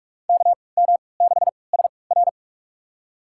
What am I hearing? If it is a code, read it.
Morse code, heard as a keyed tone, about 46 wpm, spaced as for 25 wpm, 700 Hz. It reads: KM6SR
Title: USA Call Signs - 5 Characters - 3x Character Spacing - 45wpm